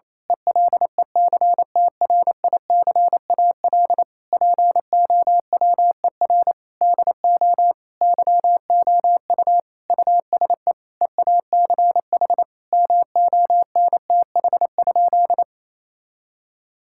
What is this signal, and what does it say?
Morse code, heard as a keyed tone, about 28 wpm, 700 Hz. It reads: ELECTRICAL POWER DO YOU USE EACH MONTH?
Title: Sentences from Top 1000 Words - Review Part 6 - 28wpm